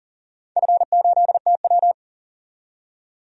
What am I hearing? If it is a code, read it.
Morse code, heard as a keyed tone, about 40 wpm, 700 Hz. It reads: F8TW
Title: DXCC-100 Call Signs - 4 Characters - 40wpm